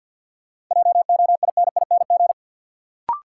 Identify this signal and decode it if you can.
Morse code, heard as a keyed tone, about 47 wpm, 700 Hz. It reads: JOINING
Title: Sets of 5 Words - New Word Drill - 50wpm